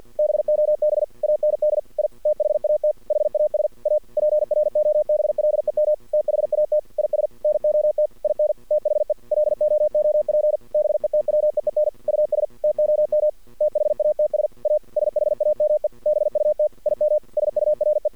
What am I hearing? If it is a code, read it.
Morse code, heard as a keyed tone, about 48 wpm, 610 Hz. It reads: BOB AND I THAT HAS A PROBLEM THAT IS NOT IN THE LOOP BETWEEN US TOM THATS A SHAME BUT IM SURE
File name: archived / CRSnet